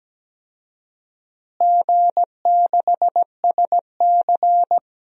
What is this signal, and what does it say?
Morse code, heard as a keyed tone, about 17 wpm, 700 Hz. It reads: G6SC
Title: DXCC-100 Call Signs - 4 Characters - Repeated 3x - 17wpm